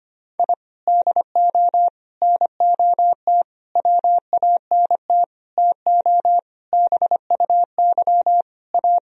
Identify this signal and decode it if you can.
Morse code, heard as a keyed tone, about 25 wpm, 700 Hz. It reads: I DO NOT WANT TO BUY A
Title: Sentences from Top 1000 Words - Review Part 4 - 25wpm